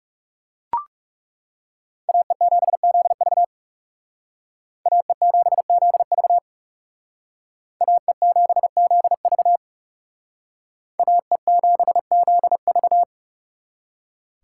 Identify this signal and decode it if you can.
Morse code, heard as a keyed tone, about 35 wpm, 700 Hz. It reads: AE7ZV AE7ZV AE7ZV AE7ZV
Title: Speed Racing - USA 2x2 Call Signs 30wpm